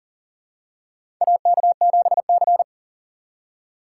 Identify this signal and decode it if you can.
Morse code, heard as a keyed tone, about 40 wpm, 700 Hz. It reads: AK7C
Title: USA Call Signs - 6x Character Spacing - 40wpm